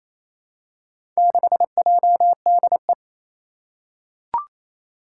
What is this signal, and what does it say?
Morse code, heard as a keyed tone, about 27 wpm, 700 Hz. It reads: 6JDE
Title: Four Challenging Letter-Number Combinations 28wpm